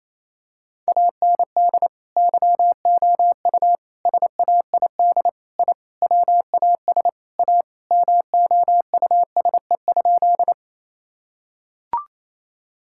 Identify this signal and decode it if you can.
Morse code, heard as a keyed tone, about 28 wpm, 700 Hz. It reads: AND YOU SAID I WAS A MOUSE?
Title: Sentences from Top 1000 Words - Review Part 5 - 28wpm